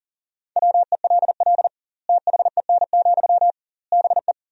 Morse code, heard as a keyed tone, about 40 wpm, 700 Hz. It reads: WELL THEN, BE
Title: 2x Word-Spacing - Sentences from Top 500 Words - 40wpm